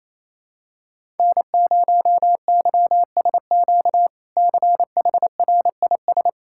Morse code, heard as a keyed tone, about 28 wpm, 700 Hz. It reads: N0YSQ CHRIS
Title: K1USN SST QSO Practice - 28wpm